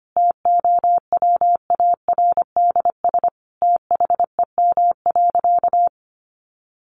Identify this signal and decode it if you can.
Morse code, heard as a keyed tone, about 25 wpm, 700 Hz. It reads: TOWARDS THEM.
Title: Sentences from Top 1100 Words 25wpm